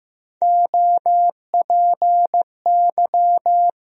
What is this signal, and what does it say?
Morse code, heard as a keyed tone, about 15 wpm, 700 Hz. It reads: OPY
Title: USA Call Sign Postfixes - 3 Letters 15wpm